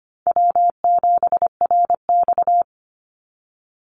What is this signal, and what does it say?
Morse code, heard as a keyed tone, about 25 wpm, 700 Hz. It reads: W7RX
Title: Call Signs - 25wpm